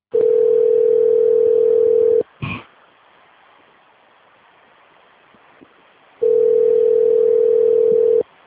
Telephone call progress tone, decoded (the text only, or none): ringback tone